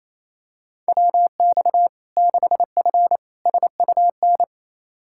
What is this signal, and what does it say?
Morse code, heard as a keyed tone, about 28 wpm, 700 Hz. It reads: WX 6F SUN
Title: Practice QSO - Exchange 2 - Info - 28wpm